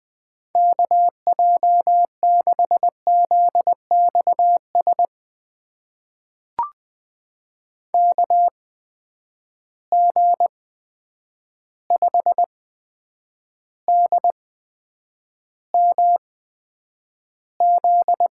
Morse code, heard as a keyed tone, about 20 wpm, spaced as for 9 wpm, 700 Hz. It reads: KJ6ZXS KG5DMZ
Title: USA Call Signs - 6 Characters - 8x Character Spacing - 20wpm